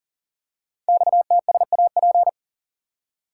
Morse code, heard as a keyed tone, about 43 wpm, 700 Hz. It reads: XTSAP
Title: Code Groups - 5 Letters - 40wpm